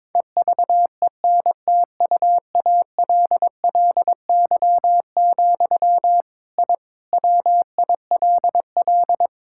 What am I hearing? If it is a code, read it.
Morse code, heard as a keyed tone, about 22 wpm, 700 Hz. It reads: EVENTUALLY, I WILL
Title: Sentences from Top 1000 Words - Review Part 6 - 22wpm